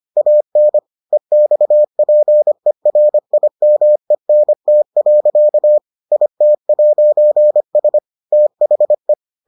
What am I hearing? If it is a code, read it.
Morse code, heard as a keyed tone, about 25 wpm, 600 Hz. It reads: AN EXPERIMENT. IT'S THE